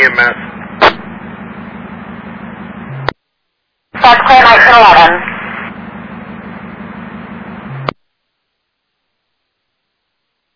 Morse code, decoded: none

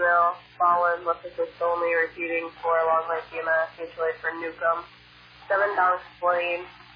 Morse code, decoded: none

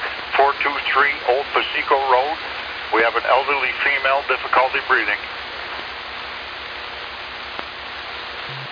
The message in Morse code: none

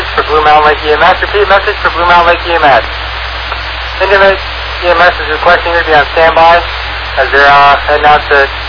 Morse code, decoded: none

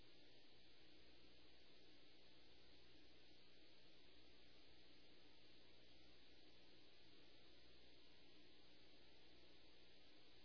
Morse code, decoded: none

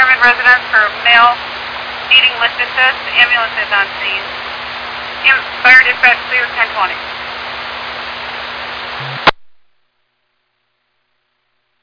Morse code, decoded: none

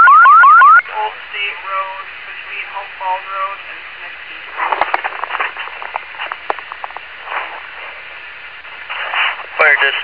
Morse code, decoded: none

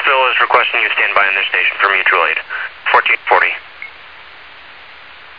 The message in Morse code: none

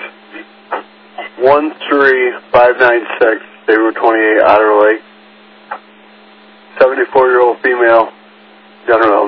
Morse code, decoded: none

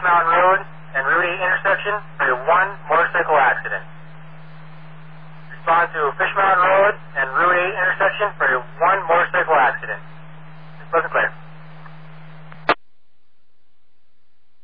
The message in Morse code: none

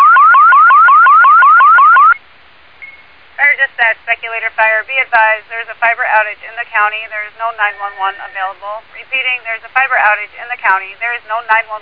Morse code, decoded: none

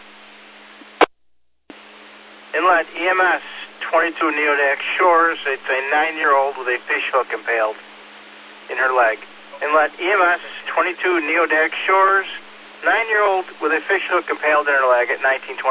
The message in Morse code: none

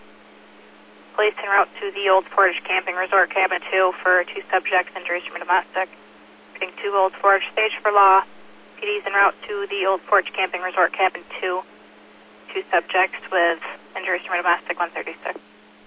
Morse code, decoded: none

